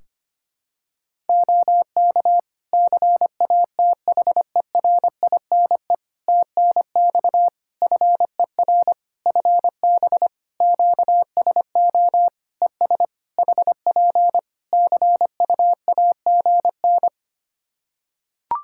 Morse code, heard as a keyed tone, about 25 wpm, 700 Hz. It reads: OK CATHERINE TNX FER FB QSO ES HP CUAGN
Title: Practice QSO - Exchange 4 - Ending - 25wpm